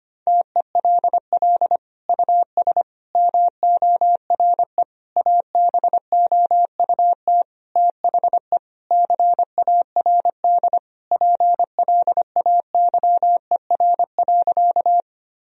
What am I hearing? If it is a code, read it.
Morse code, heard as a keyed tone, about 25 wpm, 700 Hz. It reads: TELL US MORE ABOUT THE CARD PLAYER.
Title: Sentences from Top 1000 Words - Review Part 4 - 25wpm